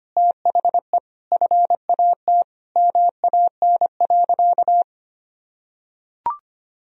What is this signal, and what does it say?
Morse code, heard as a keyed tone, about 25 wpm, 700 Hz. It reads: THE FAT MAN.
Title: Sentences from Top 1000 Words - Review Part 4 - 25wpm